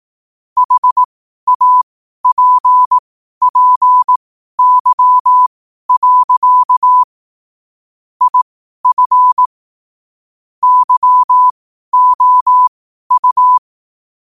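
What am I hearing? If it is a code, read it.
Morse code, heard as a keyed tone, about 18 wpm, spaced as for 12 wpm, 1000 Hz. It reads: HAPPY. IF YOU
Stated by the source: Morse code sent at 12 words per minute.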